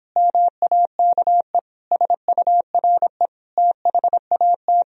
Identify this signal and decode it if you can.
Morse code, heard as a keyed tone, about 31 wpm, 700 Hz. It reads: MAKE SURE THAT